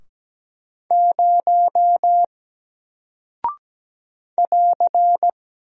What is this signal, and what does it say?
Morse code, heard as a keyed tone, about 17 wpm, 700 Hz. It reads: 0 <AR>
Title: Single Character (Letter:Number:Punctuation:Prosign) Alt 17wpm